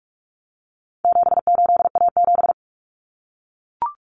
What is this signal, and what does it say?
Morse code, heard as a keyed tone, about 44 wpm, 700 Hz. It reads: 78A7
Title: Common Cut Numbers - 4-Digit 45wpm